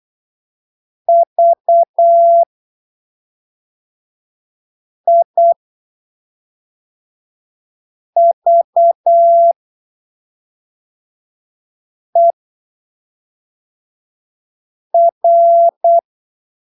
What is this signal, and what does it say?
Morse code, heard as a keyed tone, about 8 wpm, spaced as for 3 wpm, 680 Hz. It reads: VIVER
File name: arquivos / texto / texto 19 08ppm